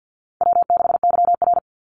Morse code, heard as a keyed tone, about 49 wpm, 700 Hz. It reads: W6YR
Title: Easy Call Signs Repeated 2x - 50wpm